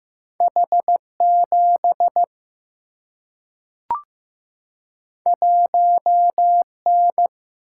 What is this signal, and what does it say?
Morse code, heard as a keyed tone, about 15 wpm, 700 Hz. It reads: H 7 1 N
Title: Single-Letter Single-Number 15wpm